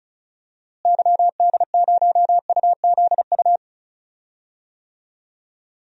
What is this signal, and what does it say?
Morse code, heard as a keyed tone, about 34 wpm, 700 Hz. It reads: YD0UZU
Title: DXCC-100 Call Signs - 6 Characters - Repeated 3x - 35wpm